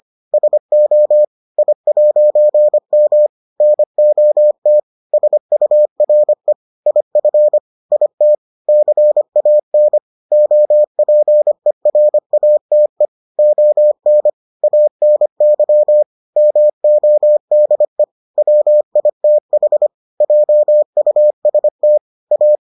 Morse code, heard as a keyed tone, about 25 wpm, 600 Hz. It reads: SO I'M NOT SURE IF IT CAN OPERATE ON ANY MODE WITH JUST A